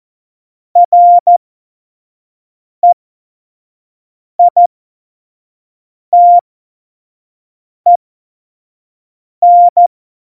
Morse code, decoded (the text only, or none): REITEN